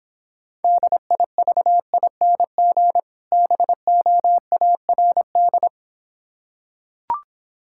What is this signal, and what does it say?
Morse code, heard as a keyed tone, about 26 wpm, 700 Hz. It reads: DIVING BOARD